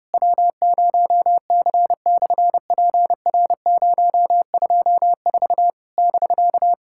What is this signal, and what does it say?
Morse code, heard as a keyed tone, about 30 wpm, 700 Hz. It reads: W0C/PR024 <BK>
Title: SOTA QSO Practice 30wpm